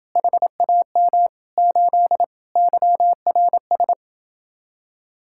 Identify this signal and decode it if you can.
Morse code, heard as a keyed tone, about 30 wpm, 700 Hz. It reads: HAM 8 YRS